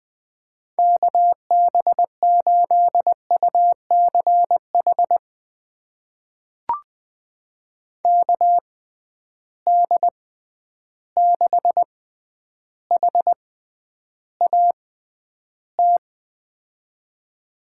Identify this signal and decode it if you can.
Morse code, heard as a keyed tone, about 20 wpm, spaced as for 10 wpm, 700 Hz. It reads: KB8UCH KD6HAT
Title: USA Call Signs - 6 Characters - 6x Character Spacing - 20wpm